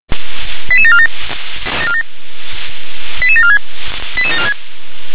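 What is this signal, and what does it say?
Five-tone selective call (ZVEI), heard as quick runs of five tones, tones of about 70 ms. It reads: 88747 88746
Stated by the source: ZVEI five-tone selective call received with an SDR.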